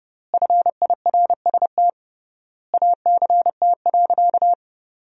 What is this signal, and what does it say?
Morse code, heard as a keyed tone, about 30 wpm, 700 Hz. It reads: FIRST ACT.
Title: 3x Word-Spacing - Sentences from Top 200 Words - 30wpm